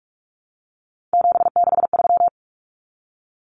Morse code, heard as a keyed tone, about 44 wpm, 700 Hz. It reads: 763
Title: Numbers - 3-Digit 45wpm